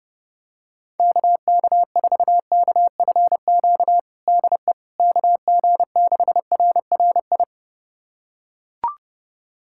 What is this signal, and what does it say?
Morse code, heard as a keyed tone, about 30 wpm, 700 Hz. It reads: KK4KFQ DE KG6RRI
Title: USA Callsign DE Callsign - 6 Characters - 30wpm